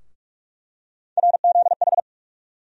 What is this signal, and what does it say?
Morse code, heard as a keyed tone, about 46 wpm, 700 Hz. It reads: RZH
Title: USA Call Sign Postfixes - 3 Letters 45wpm